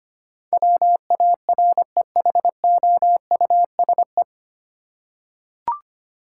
Morse code, heard as a keyed tone, about 25 wpm, 700 Hz. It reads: WAREHOUSE
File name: practice / 25wpm / 9 Letter Words - First 3 Letters Given - 25wpm